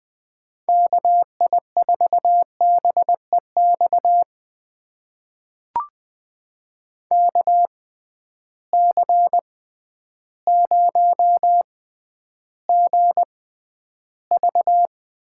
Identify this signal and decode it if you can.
Morse code, heard as a keyed tone, about 20 wpm, spaced as for 11 wpm, 700 Hz. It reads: KI4BEX KC0GV
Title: USA Call Signs - 6 Characters - 6x Character Spacing - 20wpm